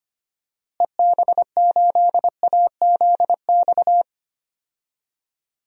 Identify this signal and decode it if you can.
Morse code, heard as a keyed tone, about 25 wpm, 700 Hz. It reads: EB8AZX
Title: DXCC-100 Call Signs - 6 Characters - Repeated 2x - 25wpm